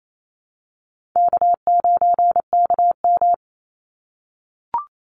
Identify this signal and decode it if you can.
Morse code, heard as a keyed tone, about 28 wpm, 700 Hz. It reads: K9KM
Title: Call Signs - 28wpm